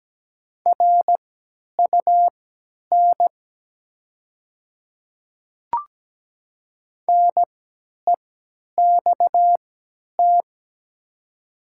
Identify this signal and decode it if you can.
Morse code, heard as a keyed tone, about 17 wpm, spaced as for 10 wpm, 700 Hz. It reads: RUN NEXT
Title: 3x Character Spacing - Top 300 Words - 17wpm